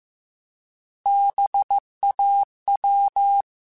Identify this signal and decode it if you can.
Morse code, heard as a keyed tone, about 15 wpm, 800 Hz. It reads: BAW
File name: Alphabetical / Alphabetical mp3s / Bs @13